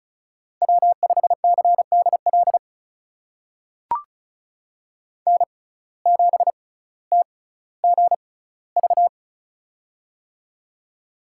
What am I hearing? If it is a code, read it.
Morse code, heard as a keyed tone, about 36 wpm, spaced as for 18 wpm, 700 Hz. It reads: W5CDL N7TGV
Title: USA Call Signs - 5 Characters - 6x Character Spacing - 35wpm